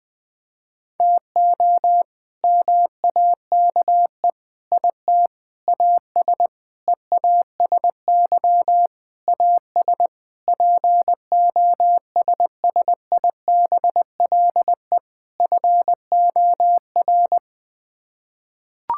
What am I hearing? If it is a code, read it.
Morse code, heard as a keyed tone, about 20 wpm, 700 Hz. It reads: TO MAKE IT AS EASY AS POSSIBLE FOR
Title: Sets of 8 Words - 20wpm